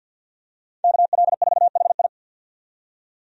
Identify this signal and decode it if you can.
Morse code, heard as a keyed tone, about 53 wpm, 700 Hz. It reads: KL4HI
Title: USA Call Signs - 5 Characters - 50wpm